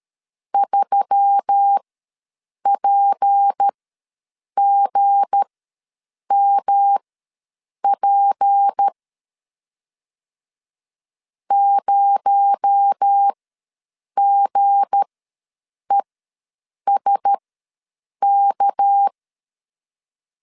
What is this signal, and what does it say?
Morse code, heard as a keyed tone, about 13 wpm, spaced as for 7 wpm, 800 Hz. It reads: 3PGMP 0GESK
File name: LEZIONI / 7 GF9Z / 7 G ws 12wpm 3140